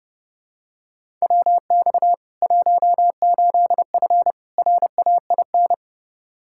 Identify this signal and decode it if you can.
Morse code, heard as a keyed tone, about 30 wpm, 700 Hz. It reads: WX 18F RAIN
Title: Practice QSO - Exchange 2 - Info - 30wpm